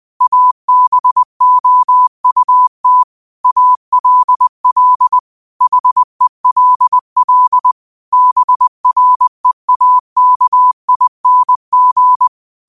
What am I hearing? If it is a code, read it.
Morse code, heard as a keyed tone, about 20 wpm, 1000 Hz. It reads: ABOUT ALL HELL BREAKING
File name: qotdfiles / 20wpm